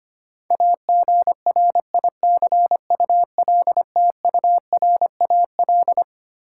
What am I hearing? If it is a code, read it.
Morse code, heard as a keyed tone, about 25 wpm, 700 Hz. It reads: AGRICULTURAL
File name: Twelve Letter Words - 25wpm